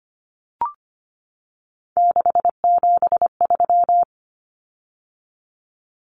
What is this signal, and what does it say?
Morse code, heard as a keyed tone, about 25 wpm, 700 Hz. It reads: 673
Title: Common Cut Numbers - 3-Digit 25wpm